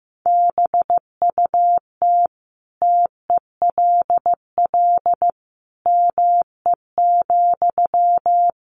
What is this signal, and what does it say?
Morse code, heard as a keyed tone, about 14 wpm, 700 Hz. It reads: BUT TELL ME,
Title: Sentences from Top 1500 Words 15wpm